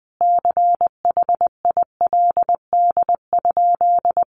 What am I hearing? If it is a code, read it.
Morse code, heard as a keyed tone, about 20 wpm, 700 Hz. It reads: CHILD?
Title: Sentences from Top 1000 Words 20wpm